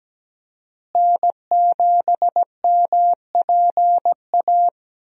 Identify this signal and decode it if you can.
Morse code, heard as a keyed tone, about 17 wpm, 700 Hz. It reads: N7MPA
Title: USA Call Signs - 5 Characters - Repeated 3x - 17wpm